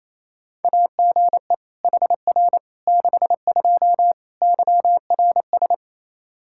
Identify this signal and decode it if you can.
Morse code, heard as a keyed tone, about 28 wpm, 700 Hz. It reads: AGE HR 62 YRS
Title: Practice QSO - Exchange 3 - More Info - 28wpm